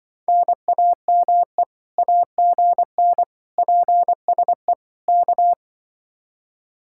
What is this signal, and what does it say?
Morse code, heard as a keyed tone, about 24 wpm, 700 Hz. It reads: NAME AGN PSE K
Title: Speed-Racing Common QSO Phrases - 20wpm